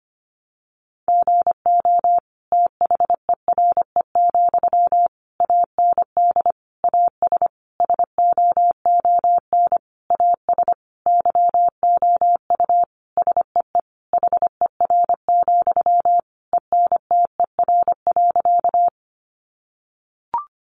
Morse code, heard as a keyed tone, about 25 wpm, 700 Hz. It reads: GO THERE, AND AS SOON AS YOU SEE HER, ENTER.
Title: Sentences from Top 1300 Words 25wpm